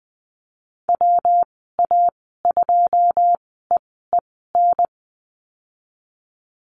Morse code, heard as a keyed tone, about 20 wpm, spaced as for 15 wpm, 700 Hz. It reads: WA2EEN
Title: 154 Lesson 25g - Letter p - Call Signs 20wpm